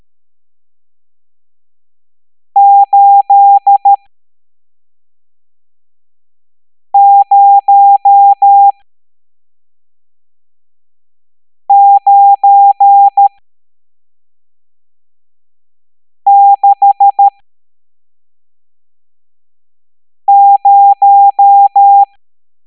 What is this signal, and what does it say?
Morse code, heard as a keyed tone, about 13 wpm, spaced as for 3 wpm, 800 Hz. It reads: 80960
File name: hamdata / L11-EX01-67890